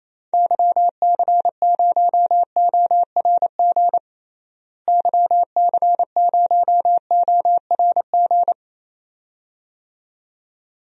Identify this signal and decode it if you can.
Morse code, heard as a keyed tone, about 28 wpm, 700 Hz. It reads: YC0ORG YC0ORG
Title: DXCC-100 Call Signs - 6 Characters - Repeated 2x - 28wpm